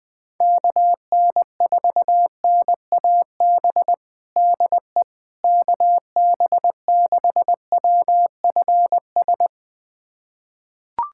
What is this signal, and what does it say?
Morse code, heard as a keyed tone, about 20 wpm, 700 Hz. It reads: KN4NAB DE KB6WFS
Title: USA Callsign DE Callsign - 6 Characters - 20wpm